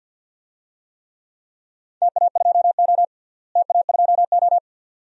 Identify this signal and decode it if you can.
Morse code, heard as a keyed tone, about 46 wpm, 700 Hz. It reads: TA2O TA2O
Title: DXCC-100 Call Signs - 4 Characters - Repeated 2x - 50wpm